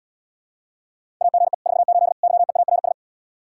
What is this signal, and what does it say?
Morse code, heard as a keyed tone, about 77 wpm, 700 Hz. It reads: AGE 48 YEARS